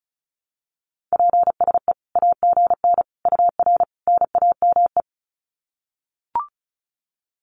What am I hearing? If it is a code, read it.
Morse code, heard as a keyed tone, about 34 wpm, 700 Hz. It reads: PSE AGN UR NAME
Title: Common QSO Phrases 35wpm